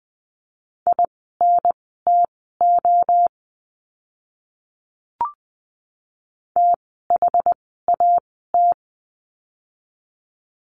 Morse code, heard as a keyed tone, about 20 wpm, spaced as for 15 wpm, 700 Hz. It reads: INTO THAT
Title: 115 Lesson 19f - Letter w - Words 20wpm